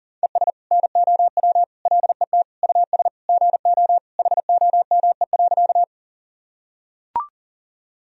Morse code, heard as a keyed tone, about 37 wpm, 700 Hz. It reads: ES NOW LET US GO HOME.
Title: Sentences from Top 300 Words and Abbreviations 40wpm